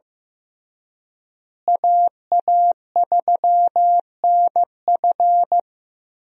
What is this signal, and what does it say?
Morse code, heard as a keyed tone, about 15 wpm, 700 Hz. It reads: AA3NF
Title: USA Call Signs - 5 Characters - Repeated 3x - 15wpm